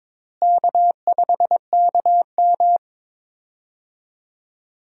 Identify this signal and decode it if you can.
Morse code, heard as a keyed tone, about 22 wpm, 700 Hz. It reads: K5KM
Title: USA Call Signs - 4 Characters - Repeated 3x - 22wpm